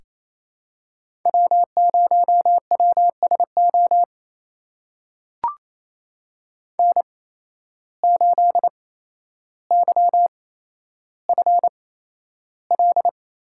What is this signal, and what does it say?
Morse code, heard as a keyed tone, about 28 wpm, spaced as for 12 wpm, 700 Hz. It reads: W0WSO N8YFL
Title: USA Call Signs - 5 Characters - 8x Character Spacing - 28wpm